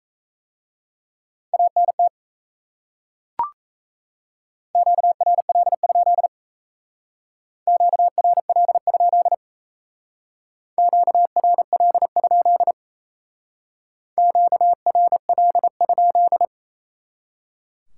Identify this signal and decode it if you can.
Morse code, heard as a keyed tone, about 34 wpm, 700 Hz. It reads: ANT QRL? QRL? QRL? QRL?
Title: Speed-Racing Top 90 QSO Elements 28wpm